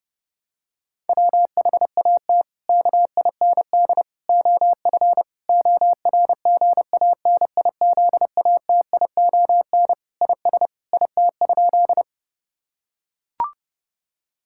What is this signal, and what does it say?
Morse code, heard as a keyed tone, about 30 wpm, 700 Hz. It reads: WHAT KIND OF ORGANIZATION IS IT?
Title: Sentences from Top 1000 Words - Review Part 5 - 30wpm